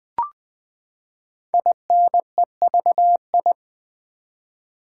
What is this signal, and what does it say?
Morse code, heard as a keyed tone, about 20 wpm, 700 Hz. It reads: INEVI
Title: Build A Word - 5 Letters At A Time - 10-Letter Words - 20wpm